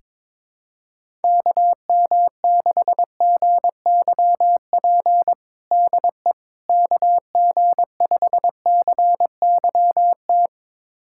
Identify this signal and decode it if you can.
Morse code, heard as a keyed tone, about 22 wpm, 700 Hz. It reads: KM6GYP DE KG5CYT
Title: USA Callsign DE Callsign - 6 Characters - 22wpm